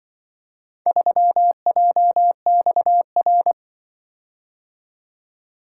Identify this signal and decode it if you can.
Morse code, heard as a keyed tone, about 23 wpm, 700 Hz. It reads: 3JXR
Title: Speed-Racing - Four Letter-Number Combinations - 20wpm